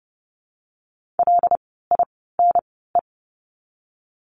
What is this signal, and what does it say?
Morse code, heard as a keyed tone, about 30 wpm, 700 Hz. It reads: L I N E
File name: course / 30:10wpm / 115 Lesson 19f - Letter w - Words 30wpm